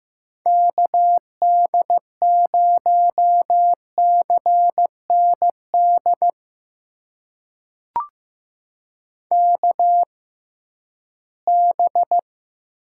Morse code, heard as a keyed tone, about 15 wpm, spaced as for 11 wpm, 700 Hz. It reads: KD0CND KB
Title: USA Call Signs - 6 Characters - 6x Character Spacing - 15wpm